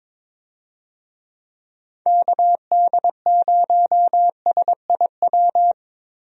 Morse code, heard as a keyed tone, about 22 wpm, 700 Hz. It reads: KD0SIW
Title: USA Call Signs - 6 Characters - Repeated 2x - 22wpm